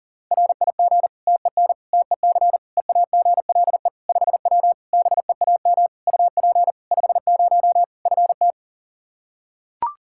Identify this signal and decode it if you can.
Morse code, heard as a keyed tone, about 37 wpm, 700 Hz. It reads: RIG TEN TEC EAGLE 5W BEAM UP 50 FT
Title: Practice QSO - Exchange 2 - Info - 40wpm